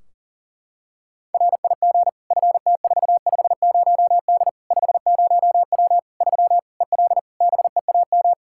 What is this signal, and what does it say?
Morse code, heard as a keyed tone, about 37 wpm, 700 Hz. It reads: RIG FT450D 50W 3 EL BEAM
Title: K1USN SST QSO Practice 40wpm